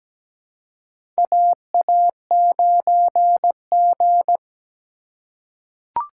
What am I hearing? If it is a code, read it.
Morse code, heard as a keyed tone, about 17 wpm, 700 Hz. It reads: AA9G
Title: USA Call Signs - 4 Characters - 2x Character Spacing - 17wpm